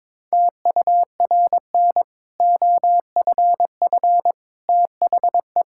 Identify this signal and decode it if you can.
Morse code, heard as a keyed tone, about 22 wpm, 700 Hz. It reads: TURN OFF THE
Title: Sentences from Top 1000 Words - Review Part 6 - 22wpm